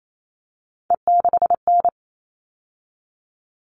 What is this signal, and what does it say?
Morse code, heard as a keyed tone, about 28 wpm, 700 Hz. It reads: E6N
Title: All Cut Numbers - 3-Digit 28wpm